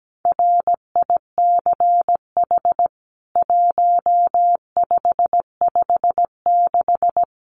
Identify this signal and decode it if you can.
Morse code, heard as a keyed tone, about 17 wpm, 700 Hz. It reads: RICH 1556
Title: CWT Exchange 17wpm